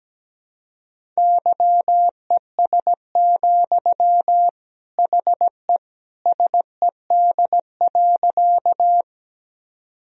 Morse code, heard as a keyed tone, about 17 wpm, 700 Hz. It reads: YES, HE SED.
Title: Sentences from Top 500 Words and Abbreviations 17wpm